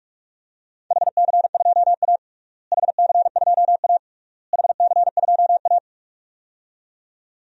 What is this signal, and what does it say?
Morse code, heard as a keyed tone, about 41 wpm, 700 Hz. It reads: HC2A HC2A HC2A
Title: DXCC-100 Call Signs - 4 Characters - Repeated 3x - 45wpm